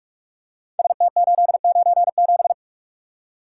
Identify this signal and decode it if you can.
Morse code, heard as a keyed tone, about 44 wpm, 700 Hz. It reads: ST897
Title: Code Groups - 5 Letter:Numbers - 45wpm